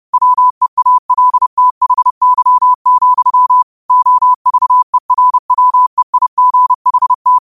Morse code, heard as a keyed tone, about 30 wpm, 1000 Hz. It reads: WEALTHY, OVERWEIGHT
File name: qotdfiles / 30wpm